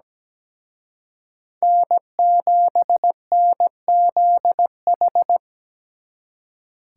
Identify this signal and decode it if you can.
Morse code, heard as a keyed tone, about 20 wpm, 700 Hz. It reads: N7NZH
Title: Speed Racing - USA Call Signs 17wpm